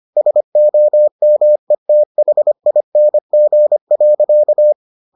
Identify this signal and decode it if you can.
Morse code, heard as a keyed tone, about 25 wpm, 600 Hz. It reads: SOMETHING.